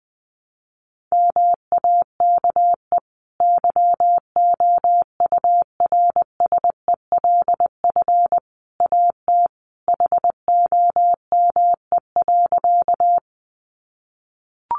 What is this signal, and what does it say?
Morse code, heard as a keyed tone, about 20 wpm, 700 Hz. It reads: MAKE YOURSELF AT HOME.
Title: Sentences from Top 600 Words 20wpm